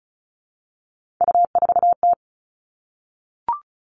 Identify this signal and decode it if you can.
Morse code, heard as a keyed tone, about 34 wpm, 700 Hz. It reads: U4T
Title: All Cut Numbers - 3-Digit 35wpm